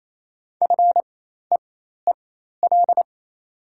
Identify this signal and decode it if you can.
Morse code, heard as a keyed tone, about 28 wpm, spaced as for 13 wpm, 690 Hz. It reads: FEEL
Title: 4x Character Spacing - Top 500 Words - 28wpm